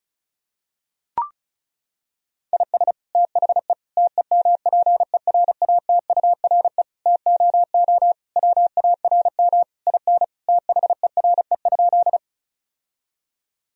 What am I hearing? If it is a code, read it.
Morse code, heard as a keyed tone, about 36 wpm, 700 Hz. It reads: IS THE TEMPERATURE TOO WARM IN THERE?
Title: Sentences from Top 1000 Words - Review Part 4 - 35wpm